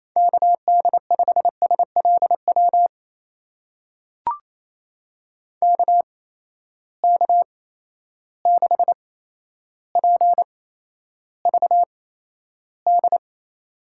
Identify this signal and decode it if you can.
Morse code, heard as a keyed tone, about 28 wpm, spaced as for 12 wpm, 700 Hz. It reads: KD5SLW KK6PVD
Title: USA Call Signs - 6 Characters - 8x Character Spacing - 28wpm